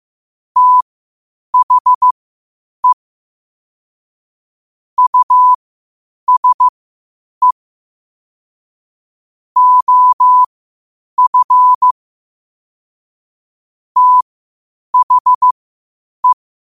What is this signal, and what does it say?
Morse code, heard as a keyed tone, about 15 wpm, spaced as for 8 wpm, 1000 Hz. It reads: THE USE OF THE
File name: qotdfiles / 8wpm